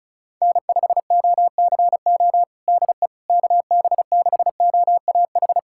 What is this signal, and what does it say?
Morse code, heard as a keyed tone, about 36 wpm, 700 Hz. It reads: N5OCO DE KB6OAH
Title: USA Callsign DE Callsign 35wpm